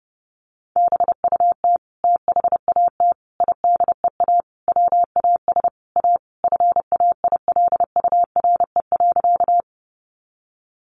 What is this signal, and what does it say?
Morse code, heard as a keyed tone, about 30 wpm, 700 Hz. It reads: BUT THAT IDEA WAS A FAILURE.
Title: Sentences from Top 900 Words 30wpm